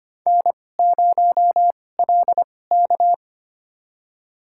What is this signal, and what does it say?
Morse code, heard as a keyed tone, about 25 wpm, spaced as for 18 wpm, 700 Hz. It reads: N0LK
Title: USA Call Signs - 4 Characters - 2x Character Spacing - 25wpm